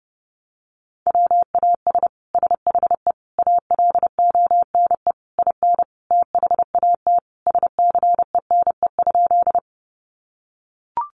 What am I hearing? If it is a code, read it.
Morse code, heard as a keyed tone, about 30 wpm, 700 Hz. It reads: WAS SHE ALONE IN THAT SCENE?
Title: Sentences from Top 1100 Words 30wpm